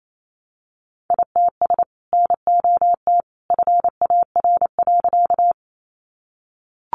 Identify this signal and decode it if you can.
Morse code, heard as a keyed tone, about 28 wpm, 700 Hz. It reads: ITS NOT FAR.
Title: Sentences from Top 600 Words 28wpm